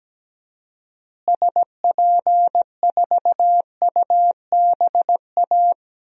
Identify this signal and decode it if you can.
Morse code, heard as a keyed tone, about 17 wpm, 700 Hz. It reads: SP4UBA
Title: DXCC-100 Call Signs - 6 Characters - Repeated 2x - 17wpm